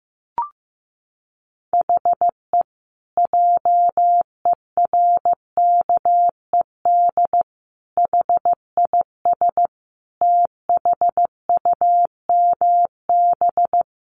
Morse code, heard as a keyed tone, about 15 wpm, 700 Hz. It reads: HE JERKED HIS THUMB
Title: A Sound Of Thunder 15wpm